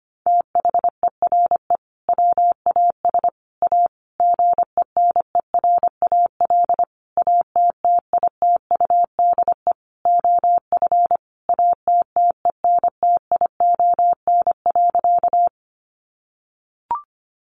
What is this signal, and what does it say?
Morse code, heard as a keyed tone, about 25 wpm, 700 Hz. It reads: THERE WAS A GENERAL ATTITUDE OF ATTENTION.
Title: Sentences from Top 1300 Words 25wpm